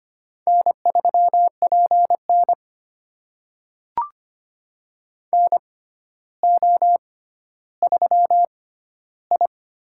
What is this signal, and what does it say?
Morse code, heard as a keyed tone, about 25 wpm, spaced as for 13 wpm, 700 Hz. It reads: N3PN NO3I
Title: USA Call Signs - 4 Characters - 6x Character Spacing - 25wpm